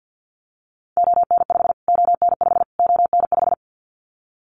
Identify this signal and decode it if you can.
Morse code, heard as a keyed tone, about 50 wpm, 700 Hz. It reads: ON5 ON5 ON5
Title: Callsign Prefixes for Top 25 DXCC - Repeated 3x - 50wpm